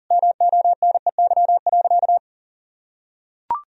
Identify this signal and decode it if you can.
Morse code, heard as a keyed tone, about 40 wpm, 700 Hz. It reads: MONEY.
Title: Sentences from Top 1000 Words - Review Part 4 - 40wpm